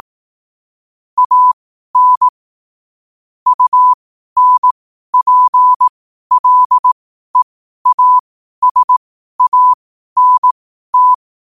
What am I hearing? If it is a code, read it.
Morse code, heard as a keyed tone, about 18 wpm, spaced as for 12 wpm, 1000 Hz. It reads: AN UNPLEASANT
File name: qotdfiles / 12wpm